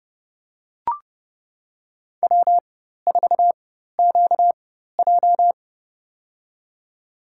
Morse code, heard as a keyed tone, about 32 wpm, spaced as for 14 wpm, 700 Hz. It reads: W4QJ
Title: USA Call Signs - 4x Character Spacing - 30wpm